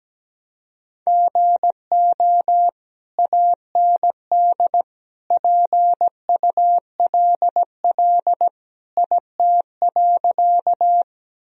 Morse code, heard as a keyed tone, about 17 wpm, 700 Hz. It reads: GO AND PULL IT.
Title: Sentences from Top 1000 Words - Review Part 2 - 17wpm